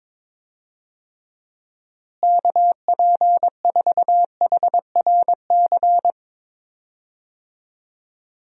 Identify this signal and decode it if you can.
Morse code, heard as a keyed tone, about 23 wpm, 700 Hz. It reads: KP4HRC
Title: USA Call Signs - 6 Characters - Repeated 2x - 22wpm